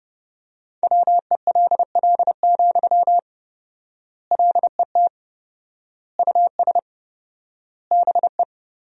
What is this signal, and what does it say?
Morse code, heard as a keyed tone, about 30 wpm, 700 Hz. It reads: WELL, LET US BE
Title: 4x Word-Spacing - Sentences from Top 300 Words - 30wpm